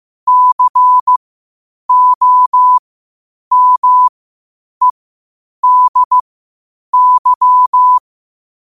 Morse code, heard as a keyed tone, about 15 wpm, spaced as for 8 wpm, 1000 Hz. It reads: COMEDY
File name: qotdfiles / 8wpm